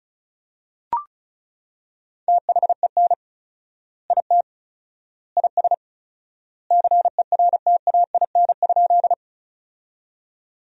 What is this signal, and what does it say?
Morse code, heard as a keyed tone, about 36 wpm, 700 Hz. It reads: THEN IT IS CERTAIN?
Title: 4x Word-Spacing - Sentences from Top 500 Words - 35wpm